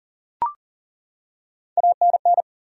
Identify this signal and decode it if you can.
Morse code, heard as a keyed tone, about 40 wpm, 700 Hz. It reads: ANN
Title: USA Common Names - Max 3 Letters - 40wpm